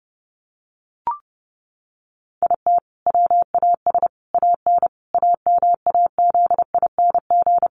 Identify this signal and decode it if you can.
Morse code, heard as a keyed tone, about 30 wpm, 700 Hz. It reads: IT WAS AN AMAZING
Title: Sentences from Top 1500 Words 30wpm